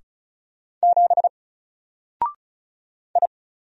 Morse code, heard as a keyed tone, about 35 wpm, 700 Hz. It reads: 7 I
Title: Single Letter-Number 35wpm